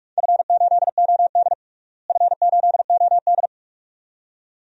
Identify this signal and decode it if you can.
Morse code, heard as a keyed tone, about 42 wpm, 700 Hz. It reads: F8OD F8OD
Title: DXCC-100 Call Signs - 4 Characters - Repeated 2x - 45wpm